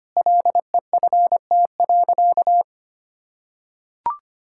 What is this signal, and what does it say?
Morse code, heard as a keyed tone, about 25 wpm, 700 Hz. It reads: LEFT.
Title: Sentences from Top 1600 Words 25wpm